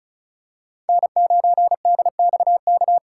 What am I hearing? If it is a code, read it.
Morse code, heard as a keyed tone, about 36 wpm, 700 Hz. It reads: N9DXK
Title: USA Call Signs - 5 Characters - 3x Character Spacing - 35wpm